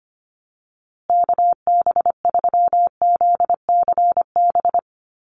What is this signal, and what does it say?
Morse code, heard as a keyed tone, about 25 wpm, 700 Hz. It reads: KB3ZCB
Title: USA Call Signs Repeated 2x 25wpm